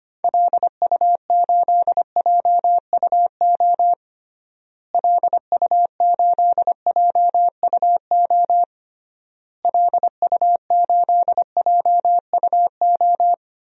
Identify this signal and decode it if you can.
Morse code, heard as a keyed tone, about 25 wpm, 700 Hz. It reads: LU8JUO LU8JUO LU8JUO
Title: DXCC-100 Call Signs - 6 Characters - Repeated 3x - 25wpm